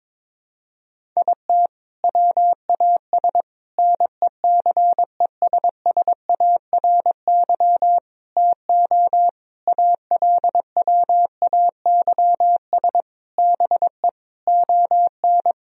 Morse code, heard as a keyed tone, about 22 wpm, 700 Hz. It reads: IT WAS NECESSARY TO ALWAYS BE ON
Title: Sentences from Top 1000 Words - Review Part 4 - 22wpm